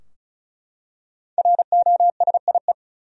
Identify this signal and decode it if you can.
Morse code, heard as a keyed tone, about 36 wpm, 700 Hz. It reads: ROSIE
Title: USA Common Names - 35wpm